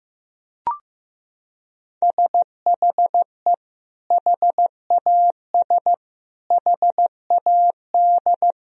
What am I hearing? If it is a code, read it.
Morse code, heard as a keyed tone, about 16 wpm, 700 Hz. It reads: SHE HAS HAD
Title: Sentences from Top 1000 Words - Review Part 4 - 15wpm